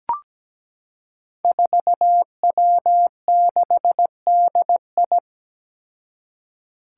Morse code, heard as a keyed tone, about 17 wpm, 700 Hz. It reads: 4W6DI
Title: Five Challenging Letter-Number Combinations 17wpm